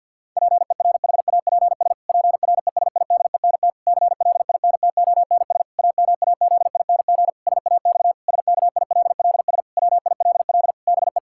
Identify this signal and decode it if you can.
Morse code, heard as a keyed tone, about 50 wpm, 700 Hz. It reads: PERHAPS PRESIDENT CLINTONS AMAZING SAX SKILLS WILL BE
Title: Pangram Challenge v2 - 50wpm